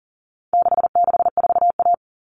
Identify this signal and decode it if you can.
Morse code, heard as a keyed tone, about 40 wpm, 700 Hz. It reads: 664A
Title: All Cut Numbers - 4-Digit 40wpm